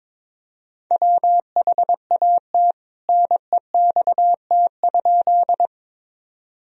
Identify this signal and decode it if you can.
Morse code, heard as a keyed tone, about 22 wpm, 700 Hz. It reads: WHAT NEXT?
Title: Sentences from Top 1000 Words - Review Part 1 - 22wpm